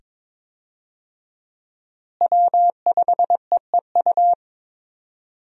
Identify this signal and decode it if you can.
Morse code, heard as a keyed tone, about 21 wpm, 700 Hz. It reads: W5EEU
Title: USA Call Signs - 5 Characters - Repeated 2x - 22wpm